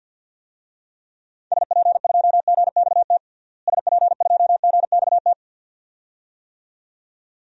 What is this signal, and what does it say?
Morse code, heard as a keyed tone, about 46 wpm, 700 Hz. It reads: SP2GXT SP2GXT
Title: DXCC-100 Call Signs - 6 Characters - Repeated 2x - 50wpm